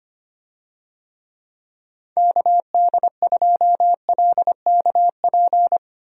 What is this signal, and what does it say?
Morse code, heard as a keyed tone, about 25 wpm, 700 Hz. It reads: KD2LKP
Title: USA Call Signs - 6 Characters - Repeated 3x - 25wpm